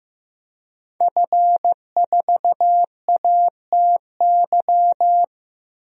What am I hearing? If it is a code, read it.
Morse code, heard as a keyed tone, about 15 wpm, 700 Hz. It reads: F4ATY
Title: DXCC-100 Call Signs - 5 Characters - 15wpm